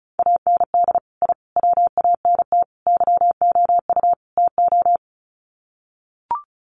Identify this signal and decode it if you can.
Morse code, heard as a keyed tone, about 34 wpm, 700 Hz. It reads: AND I WANT YOU TO
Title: Sets of 5 Words Encore 35wpm